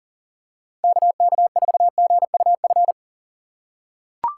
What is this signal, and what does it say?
Morse code, heard as a keyed tone, about 40 wpm, 700 Hz. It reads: KK4GUF
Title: USA Call Signs - 6 Characters - 3x Character Spacing - 40wpm